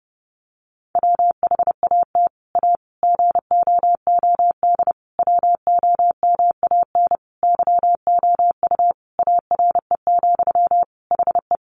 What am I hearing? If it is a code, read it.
Morse code, heard as a keyed tone, about 30 wpm, 700 Hz. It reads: WHAT A GOOD WOMAN YOU ARE, HE
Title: Sentences from Top 1100 Words 30wpm